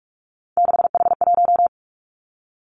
Morse code, heard as a keyed tone, about 46 wpm, 700 Hz. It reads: 6H1
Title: Three Challenging Letter-Number Combinations 45wpm